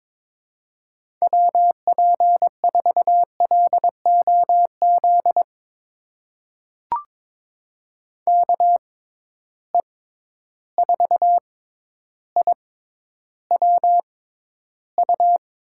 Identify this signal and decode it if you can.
Morse code, heard as a keyed tone, about 22 wpm, spaced as for 11 wpm, 700 Hz. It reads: WP4LOZ KE4IWU
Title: USA Call Signs - 6 Characters - 6x Character Spacing - 22wpm